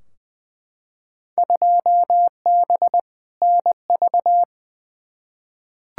Morse code, heard as a keyed tone, about 20 wpm, 700 Hz. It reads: 2B NV
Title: Field Day Exchanges Encore - 20wpm